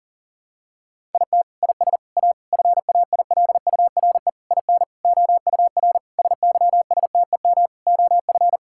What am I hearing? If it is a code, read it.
Morse code, heard as a keyed tone, about 37 wpm, 700 Hz. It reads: IT IS A FAILURE IN OUR SYSTEM OF
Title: Sentences from Top 1000 Words - Review Part 6 - 40wpm